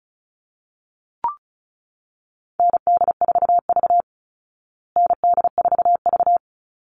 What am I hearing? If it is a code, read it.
Morse code, heard as a keyed tone, about 34 wpm, 700 Hz. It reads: ND4V ND4V
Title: Call Signs Repeated 2x - 35wpm